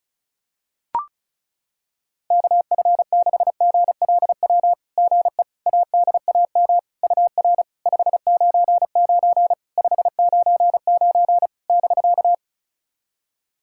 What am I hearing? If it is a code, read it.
Morse code, heard as a keyed tone, about 34 wpm, 690 Hz. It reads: KF6GLW GE ADAM UR 599 599 <BK>
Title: SOTA QSO Practice 35wpm